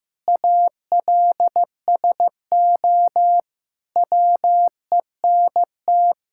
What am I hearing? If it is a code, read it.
Morse code, heard as a keyed tone, about 15 wpm, 700 Hz. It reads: ALSO WENT
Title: Sentences from Top 1600 Words 15wpm